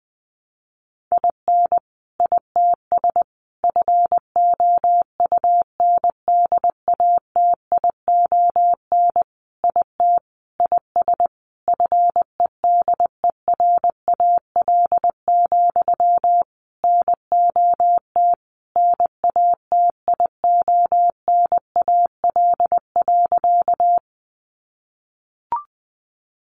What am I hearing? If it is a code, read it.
Morse code, heard as a keyed tone, about 20 wpm, 700 Hz. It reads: IN ITS FOUNDATION IT IS FEDERAL, NOT NATIONAL.
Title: Sentences from Top 1200 Words 20wpm